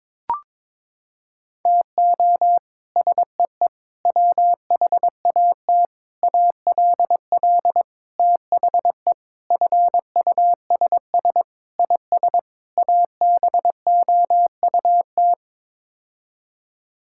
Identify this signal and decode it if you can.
Morse code, heard as a keyed tone, about 22 wpm, 700 Hz. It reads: TO SEE WHAT ALL THE FUSS IS ABOUT
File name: practice / 22wpm / Sets of 8 Words - 22wpm